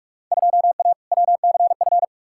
Morse code, heard as a keyed tone, about 46 wpm, 700 Hz. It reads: 2A WCF
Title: Field Day Exchanges Encore - 45wpm